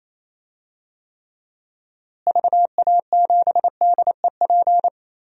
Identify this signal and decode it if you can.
Morse code, heard as a keyed tone, about 28 wpm, 700 Hz. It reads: VA7DEP
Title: DXCC-100 Call Signs - 6 Characters - Repeated 3x - 28wpm